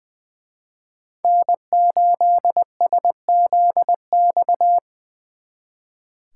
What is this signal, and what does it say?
Morse code, heard as a keyed tone, about 23 wpm, 700 Hz. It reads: N8SZX
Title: Speed Racing - USA 1x3 Call Signs 20wpm